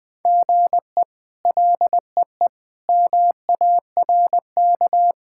Morse code, heard as a keyed tone, about 20 wpm, 700 Hz. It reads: GE LEE MARK
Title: K1USN SST QSO Practice - 20wpm